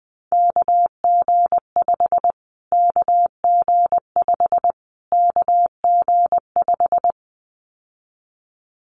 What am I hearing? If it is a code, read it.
Morse code, heard as a keyed tone, about 20 wpm, 700 Hz. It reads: KG5 KG5 KG5
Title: Callsign Prefixes for Top 25 DXCC - Repeated 3x - 20wpm